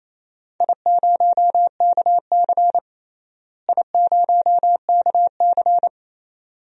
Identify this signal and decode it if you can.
Morse code, heard as a keyed tone, about 28 wpm, 700 Hz. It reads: I0KC I0KC
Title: DXCC-100 Call Signs - 4 Characters - Repeated 2x - 28wpm